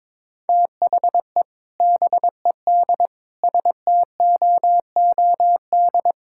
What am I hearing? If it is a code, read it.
Morse code, heard as a keyed tone, about 21 wpm, 700 Hz. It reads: THE BED STOOD
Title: Sentences from Top 1600 Words 22wpm